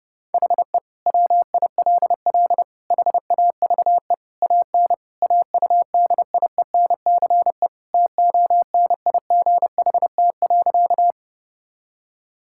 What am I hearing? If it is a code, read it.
Morse code, heard as a keyed tone, about 29 wpm, 700 Hz. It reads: HE WILL HAVE AN AUDIENCE TONIGHT.
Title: Sentences from Top 1000 Words - Review Part 5 - 30wpm